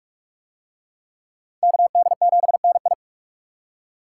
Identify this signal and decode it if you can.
Morse code, heard as a keyed tone, about 42 wpm, 700 Hz. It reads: KD7NI